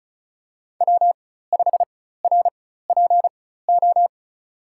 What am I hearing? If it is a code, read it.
Morse code, heard as a keyed tone, about 36 wpm, spaced as for 16 wpm, 700 Hz. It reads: W5RPO
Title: USA Call Signs - 5 Characters - 4x Character Spacing - 35wpm